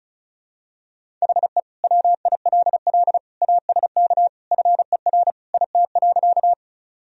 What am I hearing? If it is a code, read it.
Morse code, heard as a keyed tone, about 35 wpm, 700 Hz. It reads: HE WILL ASK FER IT.
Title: Sentences from Top 200 Words and Abbreviations 35wpm